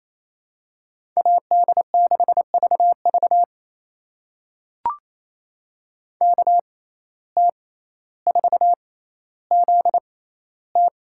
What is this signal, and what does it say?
Morse code, heard as a keyed tone, about 27 wpm, spaced as for 14 wpm, 700 Hz. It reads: AD6VV KT4ZT
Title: USA Call Signs - 5 Characters - 6x Character Spacing - 28wpm